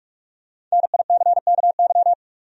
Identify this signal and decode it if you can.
Morse code, heard as a keyed tone, about 46 wpm, 700 Hz. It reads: NICKY
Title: USA Common Names - Max 5 Letters - 45wpm